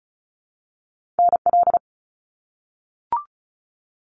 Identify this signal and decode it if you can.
Morse code, heard as a keyed tone, about 36 wpm, 700 Hz. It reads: NL
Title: Canadian Provinces and Territories - Repeated 3x - 35wpm